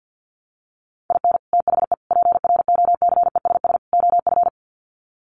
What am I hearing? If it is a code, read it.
Morse code, heard as a keyed tone, about 50 wpm, 700 Hz. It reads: IN THE PROCESS OF
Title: Sets of 4 Words 50wpm